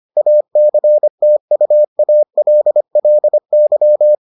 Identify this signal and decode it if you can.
Morse code, heard as a keyed tone, about 25 wpm, 600 Hz. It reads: ACTUALLY